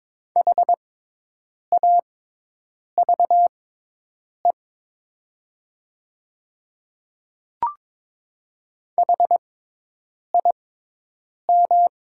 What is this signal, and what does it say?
Morse code, heard as a keyed tone, about 22 wpm, spaced as for 8 wpm, 700 Hz. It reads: HAVE HIM
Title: 6x Character Spacing - Top 100 Words - 22wpm